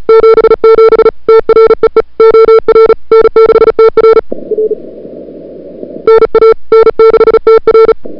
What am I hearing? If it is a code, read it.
Morse code, heard as a keyed tone, about 34 wpm, 440 Hz. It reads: 77 TREE OR N6TR R NA N6TR
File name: CWSprint / February 2007 CW Sprint / 01-2007 CW Sprint Hour1